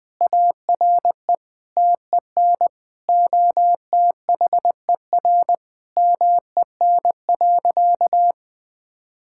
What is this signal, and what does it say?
Morse code, heard as a keyed tone, about 20 wpm, 700 Hz. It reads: ARE TEN OTHER MEN.
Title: Sentences from Top 1000 Words - Review Part 6 - 20wpm